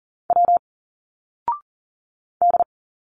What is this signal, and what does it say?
Morse code, heard as a keyed tone, about 40 wpm, 700 Hz. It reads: W D
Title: Single Letters 40wpm